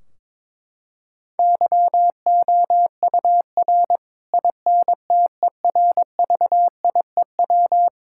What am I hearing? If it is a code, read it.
Morse code, heard as a keyed tone, about 22 wpm, 700 Hz. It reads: YOUR INTERVIEW
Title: Sentences from Top 1600 Words 22wpm